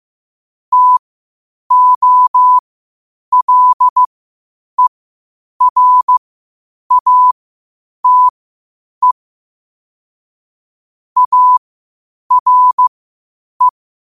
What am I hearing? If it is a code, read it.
Morse code, heard as a keyed tone, about 15 wpm, spaced as for 8 wpm, 1000 Hz. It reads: TOLERATE ARE
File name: qotdfiles / 8wpm